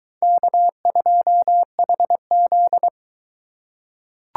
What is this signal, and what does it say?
Morse code, heard as a keyed tone, about 22 wpm, 700 Hz. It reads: K2HZ